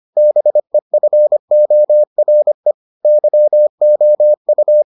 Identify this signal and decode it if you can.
Morse code, heard as a keyed tone, about 25 wpm, 600 Hz. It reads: BEFORE YOU